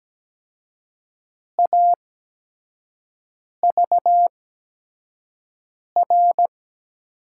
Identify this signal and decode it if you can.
Morse code, heard as a keyed tone, about 17 wpm, spaced as for 5 wpm, 700 Hz. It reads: AVR
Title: USA Call Signs - 6 Characters - 8x Character Spacing - 17wpm